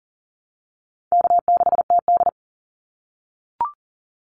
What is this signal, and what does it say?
Morse code, heard as a keyed tone, about 40 wpm, 700 Hz. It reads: K6TD
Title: Easy Call Signs - 40wpm